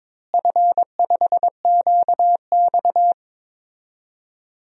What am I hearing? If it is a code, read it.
Morse code, heard as a keyed tone, about 22 wpm, 700 Hz. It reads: F5QX
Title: DXCC-100 Call Signs - 4 Characters - Repeated 2x - 22wpm